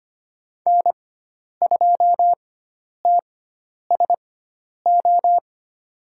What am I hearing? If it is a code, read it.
Morse code, heard as a keyed tone, about 25 wpm, spaced as for 10 wpm, 700 Hz. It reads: N2TSO
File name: Lesson 16g - Letter c - Call Signs 25wpm